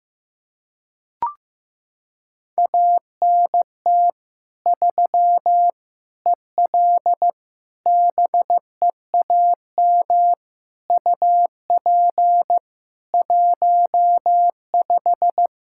Morse code, heard as a keyed tone, about 15 wpm, 700 Hz. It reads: ANT 3 EL BEAM UP 15
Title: Practice QSO - Exchange 2 - Info - 15wpm